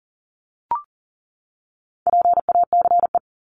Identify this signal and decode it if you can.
Morse code, heard as a keyed tone, about 40 wpm, 700 Hz. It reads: PACE
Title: Top 1751-2000 Words 40wpm